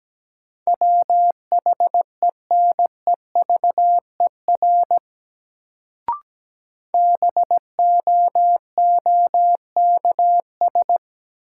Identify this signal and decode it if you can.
Morse code, heard as a keyed tone, about 17 wpm, 700 Hz. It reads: WHENEVER BOOKS
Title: Sentences from Top 1200 Words - New Word Drill Alt - 17wpm